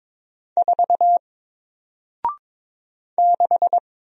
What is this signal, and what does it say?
Morse code, heard as a keyed tone, about 22 wpm, 700 Hz. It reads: 4 6
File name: Single-Letter-Number-22wpm